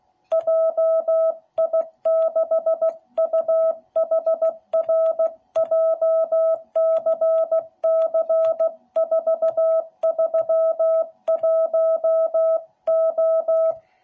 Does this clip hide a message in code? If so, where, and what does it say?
Yes: Morse code from 0.3 to 13.7 s, heard as a keyed tone, about 16 wpm, 660 Hz. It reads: JI6UHR JCC431O